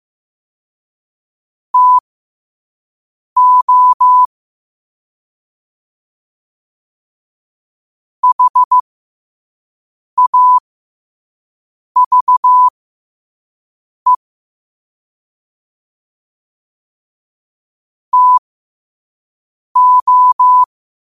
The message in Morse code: TO HAVE TO

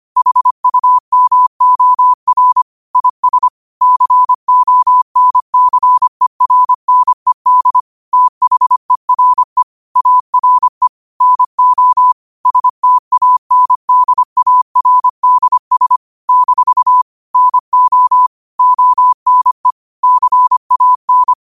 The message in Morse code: HUMOR IS CONCERNED THERE ARE NO STANDARDS - NO ONE CAN

